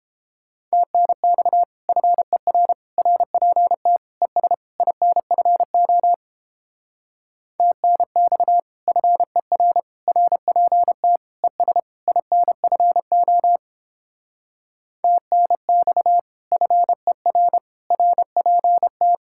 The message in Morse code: TNX FER RPT ES INFO TNX FER RPT ES INFO TNX FER RPT

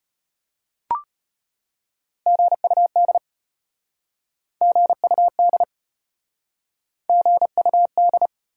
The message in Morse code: GUD GUD GUD

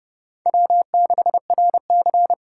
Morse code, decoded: W6RC

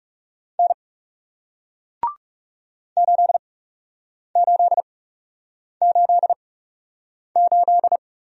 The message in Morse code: N 8888

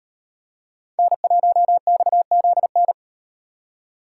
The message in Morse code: N1XZN

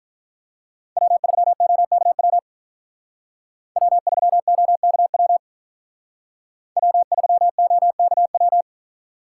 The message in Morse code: W3OKW W3OKW W3OKW

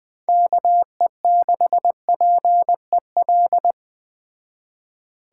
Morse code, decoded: KE6PEL

